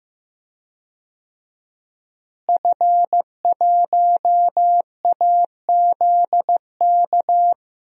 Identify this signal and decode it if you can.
Morse code, heard as a keyed tone, about 15 wpm, 700 Hz. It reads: F1AZK
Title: DXCC-100 Call Signs - 5 Characters - Repeated 2x - 15wpm